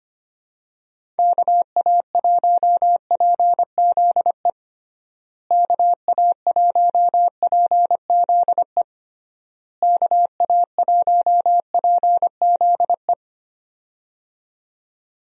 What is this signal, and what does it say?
Morse code, heard as a keyed tone, about 25 wpm, 700 Hz. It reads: KA1PZE KA1PZE KA1PZE
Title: USA Call Signs - 6 Characters - Repeated 3x - 25wpm